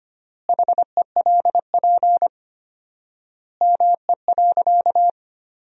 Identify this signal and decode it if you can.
Morse code, heard as a keyed tone, about 25 wpm, 700 Hz. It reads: HELP ME.
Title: 4x Word-Spacing - Sentences from Top 200 Words - 25wpm